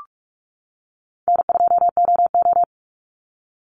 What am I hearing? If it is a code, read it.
Morse code, heard as a keyed tone, about 44 wpm, 700 Hz. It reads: N2OO
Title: Easy Call Signs - 45wpm